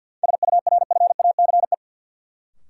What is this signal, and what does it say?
Morse code, heard as a keyed tone, about 47 wpm, 700 Hz. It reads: SURFACE
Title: Sets of 4 Words - New Word Drill - 50wpm